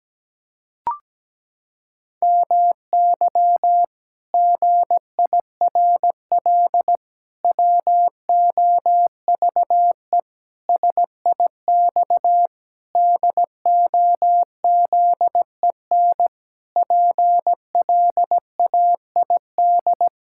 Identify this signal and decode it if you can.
Morse code, heard as a keyed tone, about 17 wpm, 700 Hz. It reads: MY GIRL WOVE SIX DOZEN PLAID
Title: Pangram Challenge v2 - 17wpm